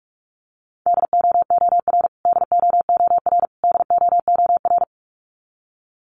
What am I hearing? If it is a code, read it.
Morse code, heard as a keyed tone, about 44 wpm, 700 Hz. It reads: DOOR DOOR DOOR
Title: Top 500 Words Repeated 45wpm